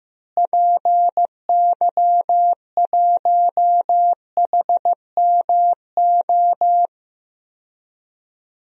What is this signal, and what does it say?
Morse code, heard as a keyed tone, about 16 wpm, 700 Hz. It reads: PY1HMO
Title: DXCC-100 Call Signs - 6 Characters - Repeated 2x - 15wpm